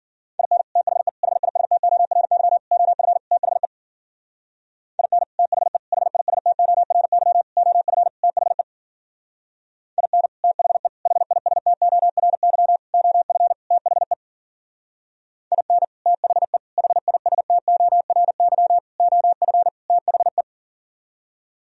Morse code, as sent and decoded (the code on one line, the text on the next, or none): .. -. / - .... / .... .. ... - --- .-. -.-- / --- ..-. / - .... / .. -. / - .... / .... .. ... - --- .-. -.-- / --- ..-. / - .... / .. -. / - .... / .... .. ... - --- .-. -.-- / --- ..-. / - .... / .. -. / - .... / .... .. ... - --- .-. -.-- / --- ..-. / - .... .
IN THE HISTORY OF THE IN THE HISTORY OF THE IN THE HISTORY OF THE IN THE HISTORY OF THE